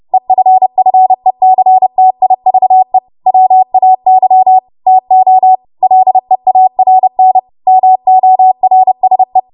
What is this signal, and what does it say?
Morse code, heard as a keyed tone, about 30 wpm, 750 Hz. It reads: EFFECTIVE WAY TO LEARN MORSE
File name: Morse / 30WPM